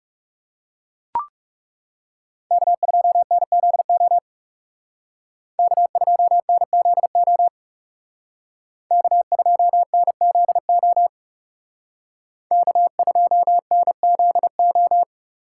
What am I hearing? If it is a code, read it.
Morse code, heard as a keyed tone, about 38 wpm, 700 Hz. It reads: K2NZO K2NZO K2NZO K2NZO